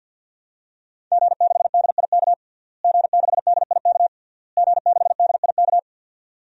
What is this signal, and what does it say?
Morse code, heard as a keyed tone, about 52 wpm, 700 Hz. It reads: G6DIK G6DIK G6DIK
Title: DXCC-100 Call Signs - 5 Characters - Repeated 3x - 50wpm